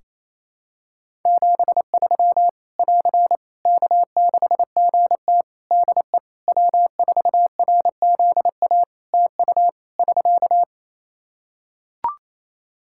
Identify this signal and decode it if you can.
Morse code, heard as a keyed tone, about 28 wpm, 700 Hz. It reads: 73 <AR> K6GT DE W4RZA TU <SK>
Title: Practice QSO - Exchange 4 - Ending - 28wpm